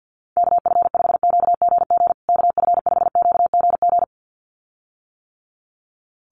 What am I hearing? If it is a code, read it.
Morse code, heard as a keyed tone, about 48 wpm, 700 Hz. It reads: KF5QGG KF5QGG
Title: USA Call Signs Repeated 2x 50wpm